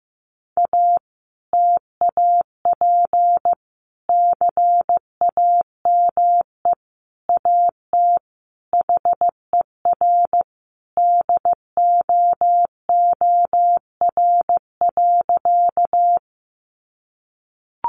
Morse code, heard as a keyed tone, about 15 wpm, 700 Hz. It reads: A TAP CAME AT HER DOOR.
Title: Sentences from Top 1400 Words 15wpm